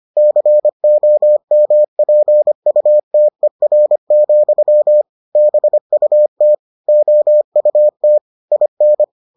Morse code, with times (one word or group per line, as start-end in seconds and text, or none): 0.2-5.0 COMPUTER,
5.4-6.5 BUT
6.9-8.2 OUT
8.5-9.0 IN